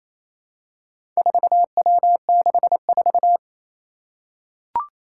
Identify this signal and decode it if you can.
Morse code, heard as a keyed tone, about 28 wpm, 700 Hz. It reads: 4W64
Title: Four Challenging Letter-Number Combinations 28wpm